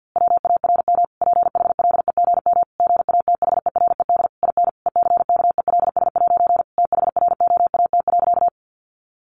Morse code, heard as a keyed tone, about 50 wpm, 700 Hz. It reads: WARM PHLEGM GATHERED IN ECKELS' THROAT.
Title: A Sound Of Thunder 50wpm